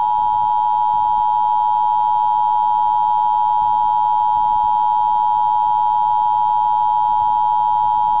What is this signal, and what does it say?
Emergency Alert System (SAME) tones (the 853 and 960 Hz attention tone). It is heard from 1.0 to 7.2 s.